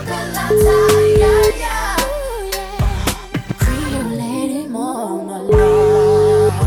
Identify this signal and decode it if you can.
Ringback tone (Europe, 425 Hz), with noise.